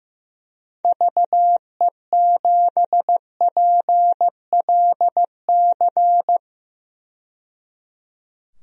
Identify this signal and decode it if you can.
Morse code, heard as a keyed tone, about 16 wpm, 700 Hz. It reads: VE7PLC
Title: DXCC-100 Call Signs - 6 Characters - 15wpm